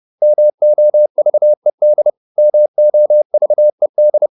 Morse code, heard as a keyed tone, about 30 wpm, 600 Hz. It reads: MOVED MOVED